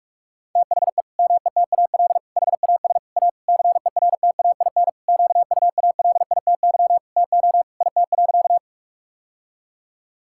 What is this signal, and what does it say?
Morse code, heard as a keyed tone, about 46 wpm, 700 Hz. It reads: THE METAL HAS A CERTAIN QUALITY TO IT.
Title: Sentences from Top 1000 Words - Review Part 6 - 45wpm